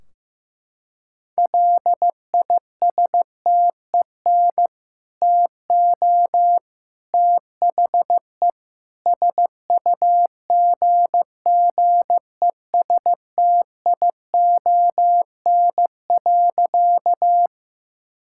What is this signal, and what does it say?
Morse code, heard as a keyed tone, about 15 wpm, 700 Hz. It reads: LISTEN TO THE SUGGESTION.
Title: Sentences from Top 1600 Words 15wpm